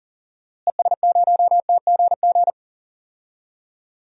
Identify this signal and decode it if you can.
Morse code, heard as a keyed tone, about 37 wpm, 700 Hz. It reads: ES0TGG
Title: DXCC-100 Call Signs - 6 Characters - Repeated 3x - 40wpm